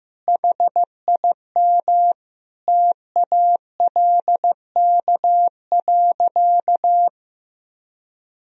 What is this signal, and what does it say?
Morse code, heard as a keyed tone, about 15 wpm, 700 Hz. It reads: HIM TALK.
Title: Sentences from Top 1000 Words - Review Part 4 - 15wpm